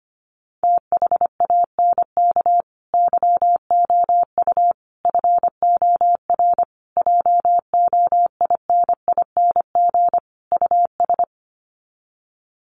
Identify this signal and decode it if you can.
Morse code, heard as a keyed tone, about 25 wpm, 700 Hz. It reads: THANK YOU FOR JOINING US
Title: Sets of 5 Words 25wpm